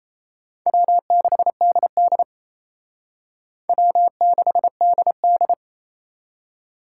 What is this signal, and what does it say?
Morse code, heard as a keyed tone, about 34 wpm, 700 Hz. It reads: W6DD W6DD
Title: Speed Racing - USA 1x2 Call Signs 28wpm